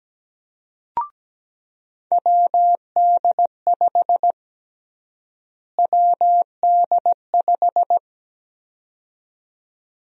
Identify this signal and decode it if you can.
Morse code, heard as a keyed tone, about 17 wpm, 700 Hz. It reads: WD5 WD5
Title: USA Call Sign Prefixes - Repeated 2x - 17wpm